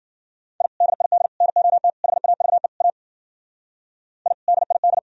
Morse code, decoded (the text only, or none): I DID NOT HAVE A I DID